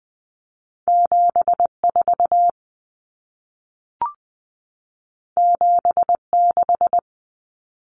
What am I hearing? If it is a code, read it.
Morse code, heard as a keyed tone, about 20 wpm, 700 Hz. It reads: 74 76
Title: Common Cut Numbers - 2-Digit 20wpm